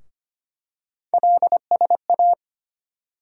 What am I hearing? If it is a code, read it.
Morse code, heard as a keyed tone, about 25 wpm, 700 Hz. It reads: LSA
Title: USA Call Sign Postfixes - 3 Letters 25wpm